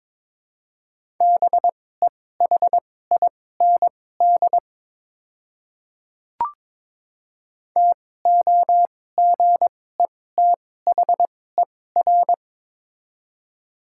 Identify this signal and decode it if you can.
Morse code, heard as a keyed tone, about 22 wpm, spaced as for 16 wpm, 700 Hz. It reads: BEHIND TOGETHER
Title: 2x Character Spacing - Top 500 Words - 22wpm